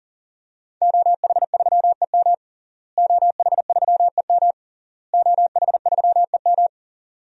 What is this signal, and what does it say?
Morse code, heard as a keyed tone, about 40 wpm, 700 Hz. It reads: OH3EM OH3EM OH3EM
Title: DXCC-100 Call Signs - 5 Characters - Repeated 3x - 40wpm